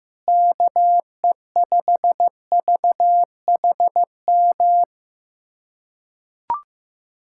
Morse code, heard as a keyed tone, about 15 wpm, 700 Hz. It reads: KE5VHM
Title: USA Callsign DE Callsign 15wpm